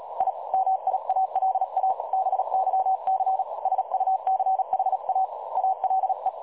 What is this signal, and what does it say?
Morse code, heard as a keyed tone, about 38 wpm, 760 Hz. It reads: I MIALA BYC SUCHA ALE
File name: dlazaawansowanych / nagraniazpasma / websdr recording 3565.0kHz